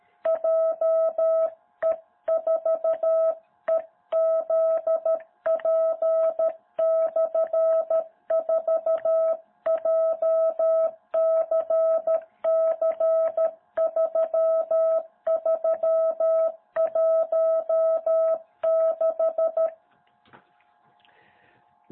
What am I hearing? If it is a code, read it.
Morse code, heard as a keyed tone, about 13 wpm, 660 Hz. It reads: JE4EZP/4 JCC3316
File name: jh8cbh / 2021cw